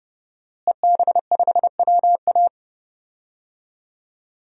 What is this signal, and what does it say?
Morse code, heard as a keyed tone, about 32 wpm, 700 Hz. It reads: EB5WA
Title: DXCC-100 Call Signs - 5 Characters - Repeated 3x - 30wpm